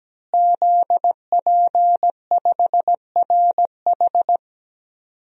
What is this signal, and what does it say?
Morse code, heard as a keyed tone, about 17 wpm, 700 Hz. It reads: ZP5RH
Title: DXCC-100 Call Signs - 5 Characters - Repeated 3x - 17wpm